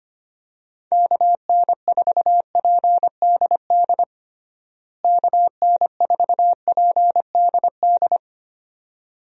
Morse code, heard as a keyed tone, about 25 wpm, 700 Hz. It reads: KN4PDD KN4PDD
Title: USA Call Signs - 6 Characters - Repeated 2x - 25wpm